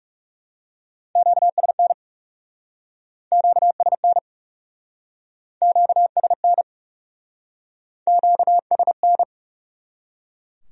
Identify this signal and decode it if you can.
Morse code, heard as a keyed tone, about 38 wpm, 700 Hz. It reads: QSN QSN QSN QSN